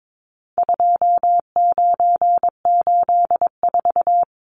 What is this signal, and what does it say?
Morse code, heard as a keyed tone, about 22 wpm, 700 Hz. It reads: 2984
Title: Numbers - 4-Digit 22wpm